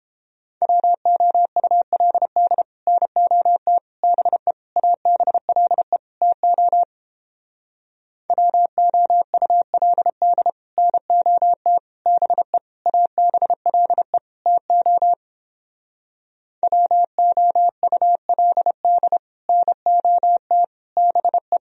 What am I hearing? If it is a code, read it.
Morse code, heard as a keyed tone, about 29 wpm, 700 Hz. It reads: WOULD NOT BE ABLE TO WOULD NOT BE ABLE TO WOULD NOT BE